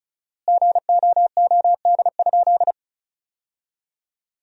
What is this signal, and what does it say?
Morse code, heard as a keyed tone, about 34 wpm, 700 Hz. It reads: GOOD?
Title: 4x Word-Spacing - Sentences from Top 200 Words - 35wpm